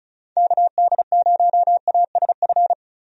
Morse code, heard as a keyed tone, about 36 wpm, 700 Hz. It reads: KD0ASF
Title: USA Call Signs - 6 Characters - Repeated 3x - 35wpm